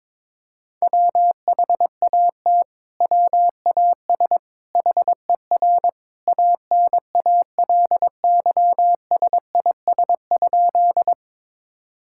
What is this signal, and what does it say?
Morse code, heard as a keyed tone, about 22 wpm, 700 Hz. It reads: WHAT WAS HER ANALYSIS?
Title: Sentences from Top 1000 Words - Review Part 5 - 22wpm